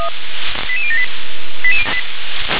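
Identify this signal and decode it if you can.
Five-tone selective call (ZVEI), heard as quick runs of five tones, tones of about 70 ms. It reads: 88078
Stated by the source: ZVEI five-tone selective call 88018 received on SDR.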